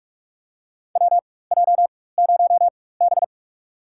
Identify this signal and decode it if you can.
Morse code, heard as a keyed tone, about 43 wpm, spaced as for 21 wpm, 700 Hz. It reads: WJ0B
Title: USA Call Signs - 4 Characters - 4x Character Spacing - 45wpm